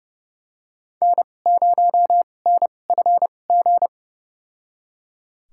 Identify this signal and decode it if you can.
Morse code, heard as a keyed tone, about 30 wpm, spaced as for 22 wpm, 700 Hz. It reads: N0NFG
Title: USA Call Signs - 5 Characters - 2x Character Spacing - 30wpm